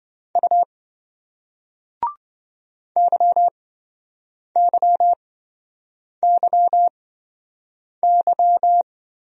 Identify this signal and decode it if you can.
Morse code, heard as a keyed tone, about 26 wpm, spaced as for 7 wpm, 700 Hz. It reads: U YYYY